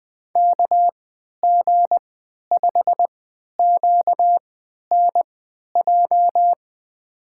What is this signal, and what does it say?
Morse code, heard as a keyed tone, about 21 wpm, spaced as for 11 wpm, 700 Hz. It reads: KG5QNJ
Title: USA Call Signs - 6 Characters - 3x Character Spacing - 20wpm